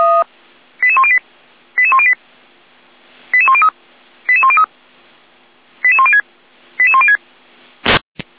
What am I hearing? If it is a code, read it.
Five-tone selective call (ZVEI), heard as quick runs of five tones, tones of about 70 ms. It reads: 88108 88108 88103 88103 88106 88106
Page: /455/SDR/ZVEI/